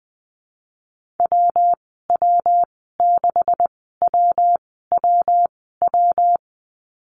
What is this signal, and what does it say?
Morse code, heard as a keyed tone, about 20 wpm, spaced as for 14 wpm, 700 Hz. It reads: WW6WWW
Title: 140 Lesson 23g - Letter f - Call Signs 20wpm